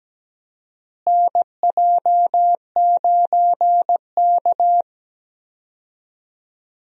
Morse code, heard as a keyed tone, about 17 wpm, 700 Hz. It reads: NJ9K
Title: USA Call Signs - 4 Characters - Repeated 2x - 17wpm